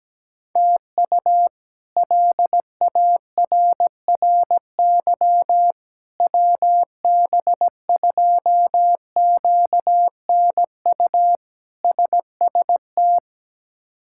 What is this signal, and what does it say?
Morse code, heard as a keyed tone, about 17 wpm, 700 Hz. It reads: TU LARRY WB2QNU SST
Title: K1USN SST QSO Practice - 17wpm